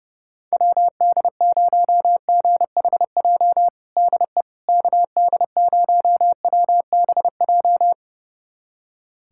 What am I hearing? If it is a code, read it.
Morse code, heard as a keyed tone, about 30 wpm, 700 Hz. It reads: WD0GHJ DE KD0WBJ
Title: USA Callsign DE Callsign - 6 Characters - 30wpm